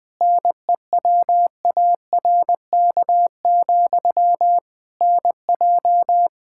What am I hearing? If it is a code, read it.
Morse code, heard as a keyed tone, about 20 wpm, 700 Hz. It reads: NEWARK, NJ
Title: Practice QSO - Exchange 1 - Beginning - 20wpm